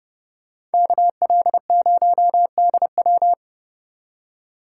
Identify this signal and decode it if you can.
Morse code, heard as a keyed tone, about 31 wpm, 700 Hz. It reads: KL0DW